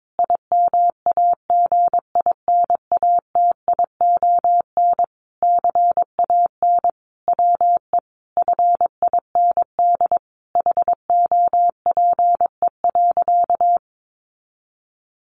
Morse code, heard as a keyed tone, about 22 wpm, 700 Hz. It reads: IMAGINATION CAN WE FIND HOPE.
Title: Sentences from Top 1300 Words 22wpm